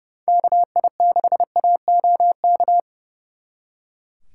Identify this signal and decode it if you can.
Morse code, heard as a keyed tone, about 39 wpm, 700 Hz. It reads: KI6AOK